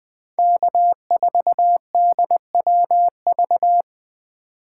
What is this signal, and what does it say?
Morse code, heard as a keyed tone, about 23 wpm, 700 Hz. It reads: K4DWV
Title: Speed Racing - USA 1x3 Call Signs 20wpm